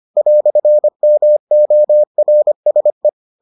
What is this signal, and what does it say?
Morse code, heard as a keyed tone, about 25 wpm, 600 Hz. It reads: "MORSE